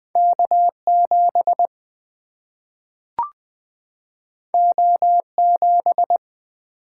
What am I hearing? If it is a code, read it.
Morse code, heard as a keyed tone, about 20 wpm, 700 Hz. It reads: K7 O7
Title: Single-Letter Single-Number 20wpm